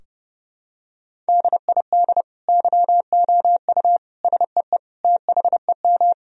Morse code, heard as a keyed tone, about 30 wpm, 700 Hz. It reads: DID YOU SEE THEM
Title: Sentences from Top 1000 Words - Review Part 2 - 30wpm